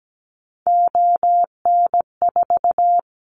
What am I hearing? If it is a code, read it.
Morse code, heard as a keyed tone, about 17 wpm, 700 Hz. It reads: ON4
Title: Callsign Prefixes for Top 50 DXCC - 17wpm